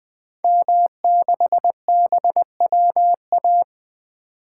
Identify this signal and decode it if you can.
Morse code, heard as a keyed tone, about 20 wpm, 700 Hz. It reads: M6BWA
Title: 2023 SOTA Activator Call Signs - 20wpm